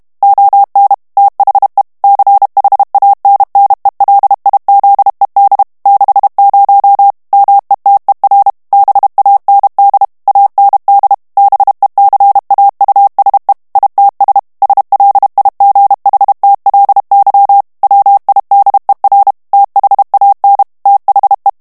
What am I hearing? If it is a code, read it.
Morse code, heard as a keyed tone, about 32 wpm, 800 Hz. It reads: ON THE CHANNELIZED 60 METER BAND AND BECAUSE ITS SLIGHTLY WIDER THAN THE
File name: morse mp3 / ANGLAIS texte clair 3 vitesse 30 mots minute